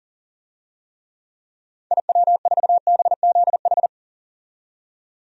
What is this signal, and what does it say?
Morse code, heard as a keyed tone, about 40 wpm, 700 Hz. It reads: IW4BZH
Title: DXCC-100 Call Signs - 6 Characters - 40wpm